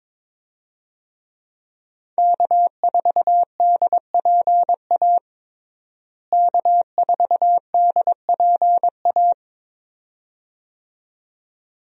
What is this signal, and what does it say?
Morse code, heard as a keyed tone, about 22 wpm, 700 Hz. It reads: K4DPA K4DPA
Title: USA Call Signs - 5 Characters - Repeated 2x - 22wpm